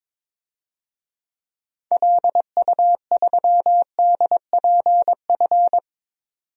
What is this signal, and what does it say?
Morse code, heard as a keyed tone, about 21 wpm, 700 Hz. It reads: LU3DPF
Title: DXCC-100 Call Signs - 6 Characters - 22wpm